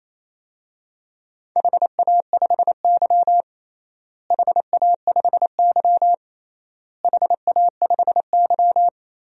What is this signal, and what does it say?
Morse code, heard as a keyed tone, about 27 wpm, 700 Hz. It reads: HA5Y HA5Y HA5Y
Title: DXCC-100 Call Signs - 4 Characters - Repeated 3x - 28wpm